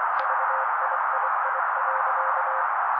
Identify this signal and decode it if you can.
Morse code, heard as a keyed tone, about 24 wpm, 530 Hz. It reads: VIII.
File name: daily / 23wpm noise